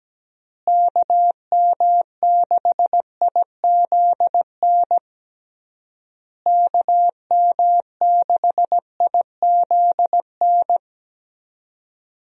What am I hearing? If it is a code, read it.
Morse code, heard as a keyed tone, about 17 wpm, 700 Hz. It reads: KM6IZN KM6IZN
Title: USA Call Signs - 6 Characters - Repeated 3x - 17wpm